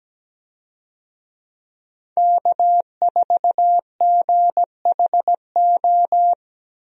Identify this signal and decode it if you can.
Morse code, heard as a keyed tone, about 17 wpm, 700 Hz. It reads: K4GHO
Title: USA Call Signs - 5 Characters - Repeated 2x - 17wpm